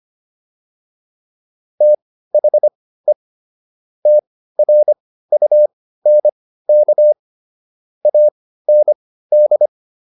Morse code, heard as a keyed tone, about 25 wpm, spaced as for 15 wpm, 600 Hz. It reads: THE TRUNK AND